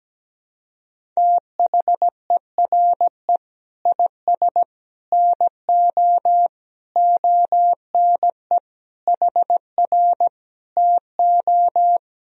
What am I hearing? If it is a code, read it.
Morse code, heard as a keyed tone, about 17 wpm, 700 Hz. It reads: THERE IS NO ONE HR TO
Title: Sentences from Top 200 Words and Abbreviations 17wpm